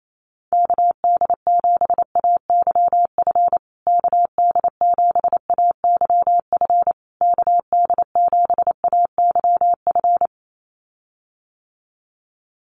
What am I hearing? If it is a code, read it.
Morse code, heard as a keyed tone, about 28 wpm, 700 Hz. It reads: KD7AYF KD7AYF KD7AYF
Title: USA Call Signs Repeated 3x 28wpm